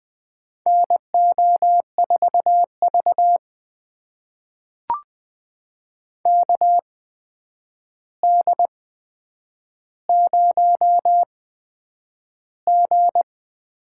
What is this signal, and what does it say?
Morse code, heard as a keyed tone, about 20 wpm, spaced as for 9 wpm, 700 Hz. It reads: NO4V KD0G
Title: USA Call Signs - 8x Character Spacing - 20wpm